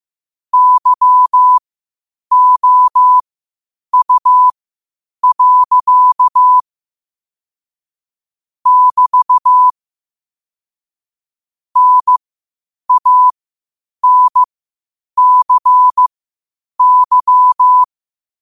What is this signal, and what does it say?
Morse code, heard as a keyed tone, about 15 wpm, spaced as for 8 wpm, 1000 Hz. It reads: YOU. <BT> NANCY
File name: qotdfiles / 8wpm